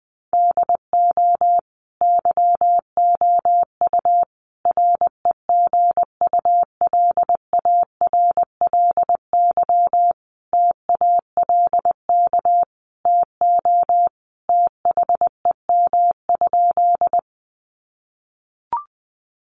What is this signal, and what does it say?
Morse code, heard as a keyed tone, about 19 wpm, 700 Hz. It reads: DO YOU REGULARLY TALK TO THEM?
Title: Sentences from Top 1400 Words 20wpm